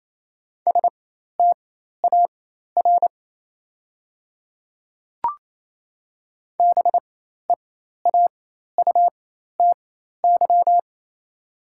Morse code, heard as a keyed tone, about 28 wpm, spaced as for 13 wpm, 700 Hz. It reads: STAR BEAUTY
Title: 4x Character Spacing - Top 500 Words - 28wpm